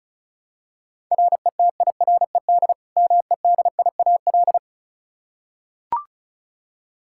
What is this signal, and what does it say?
Morse code, heard as a keyed tone, about 36 wpm, 700 Hz. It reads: RETIRED MEDIAL
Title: Practice QSO - Exchange 3 - More Info - 35wpm